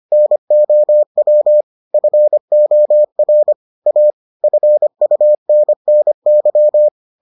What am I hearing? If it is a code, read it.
Morse code, heard as a keyed tone, about 25 wpm, 600 Hz. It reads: NOW FOR A FUNNY